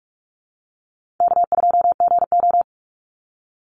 Morse code, heard as a keyed tone, about 44 wpm, 700 Hz. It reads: K2GO
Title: Easy Call Signs - 45wpm